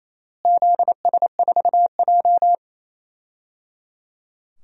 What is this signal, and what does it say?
Morse code, heard as a keyed tone, about 28 wpm, 700 Hz. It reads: ZS4J
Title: DXCC-100 Call Signs - 4 Characters - Repeated 3x - 28wpm